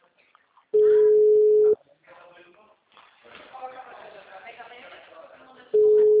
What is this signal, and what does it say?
Ringback tone (Europe, 425 Hz), with speech.